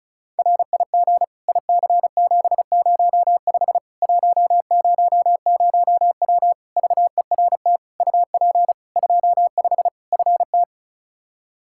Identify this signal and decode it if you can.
Morse code, heard as a keyed tone, about 34 wpm, 700 Hz. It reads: RIG IC705 100W VERT UP 25 FT
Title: K1USN SST QSO Practice 35wpm